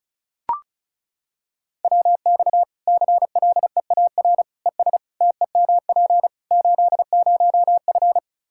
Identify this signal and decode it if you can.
Morse code, heard as a keyed tone, about 36 wpm, 700 Hz. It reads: WX CLEAR ES TEMP 80F
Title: K1USN SST QSO Practice 35wpm